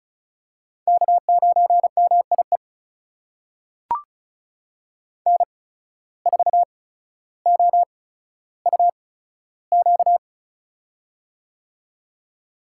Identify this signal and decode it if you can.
Morse code, heard as a keyed tone, about 34 wpm, spaced as for 15 wpm, 700 Hz. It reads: K9MIE N4OUQ
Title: USA Call Signs - 5 Characters - 8x Character Spacing - 35wpm